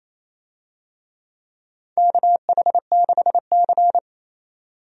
Morse code, heard as a keyed tone, about 28 wpm, 700 Hz. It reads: KH6C
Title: DXCC-100 Call Signs - 4 Characters - 28wpm